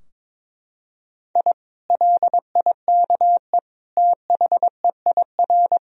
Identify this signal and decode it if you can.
Morse code, heard as a keyed tone, about 22 wpm, 700 Hz. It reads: I LIKE THEIR
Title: Sentences from Top 1000 Words - Review Part 4 - 22wpm